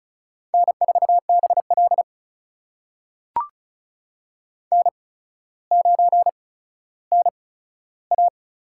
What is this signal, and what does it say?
Morse code, heard as a keyed tone, about 36 wpm, spaced as for 15 wpm, 700 Hz. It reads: N4BL N9NA
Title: USA Call Signs - 4 Characters - 8x Character Spacing - 35wpm